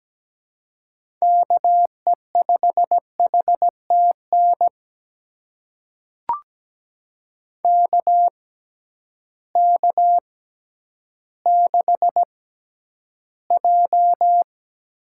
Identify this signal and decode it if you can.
Morse code, heard as a keyed tone, about 17 wpm, spaced as for 10 wpm, 700 Hz. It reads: KE5HTN KK6J
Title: USA Call Signs - 6x Character Spacing - 17wpm